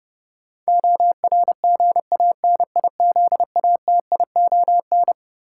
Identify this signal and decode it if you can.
Morse code, heard as a keyed tone, about 30 wpm, 700 Hz. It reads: ORGANIZATION
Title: Sentences from Top 1100 Words - New Word Drill Alt - 30wpm